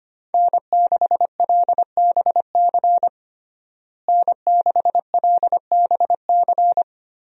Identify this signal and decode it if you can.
Morse code, heard as a keyed tone, about 25 wpm, 700 Hz. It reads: N6LBC N6LBC
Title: USA Call Signs - 5 Characters - Repeated 3x - 25wpm